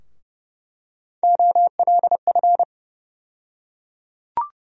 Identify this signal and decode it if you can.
Morse code, heard as a keyed tone, about 30 wpm, 700 Hz. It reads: OLF
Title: USA Call Sign Postfixes - 3 Letters 30wpm